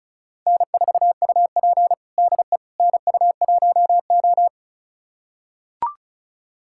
Morse code, heard as a keyed tone, about 36 wpm, 700 Hz. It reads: N4UP DE NU1O
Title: USA Callsign DE Callsign - 4 Characters - 35wpm